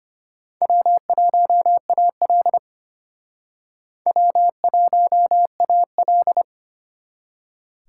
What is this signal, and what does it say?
Morse code, heard as a keyed tone, about 31 wpm, 700 Hz. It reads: W1AL W1AL
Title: Speed Racing - USA Call Signs 25wpm